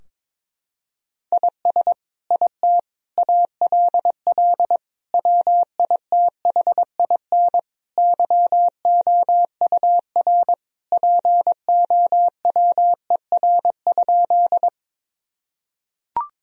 Morse code, heard as a keyed tone, about 22 wpm, 700 Hz. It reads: IS IT ALL WITHIN YOUR POWER?
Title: Sentences from Top 1000 Words - Review Part 2 - 22wpm